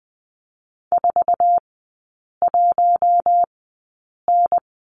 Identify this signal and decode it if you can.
Morse code, heard as a keyed tone, about 20 wpm, spaced as for 8 wpm, 700 Hz. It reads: 41N
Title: 133 Lesson 22e - Question Mark - Three Characters 20wpm